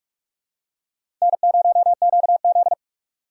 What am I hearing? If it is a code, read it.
Morse code, heard as a keyed tone, about 46 wpm, 700 Hz. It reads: N0QZ
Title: USA Call Signs - 4 Characters - 2x Character Spacing - 45wpm